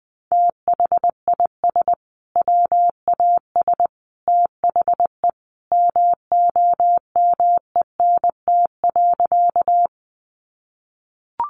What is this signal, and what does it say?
Morse code, heard as a keyed tone, about 20 wpm, 700 Hz. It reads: THIS WAS THE MOMENT.
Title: Sentences from Top 1000 Words 20wpm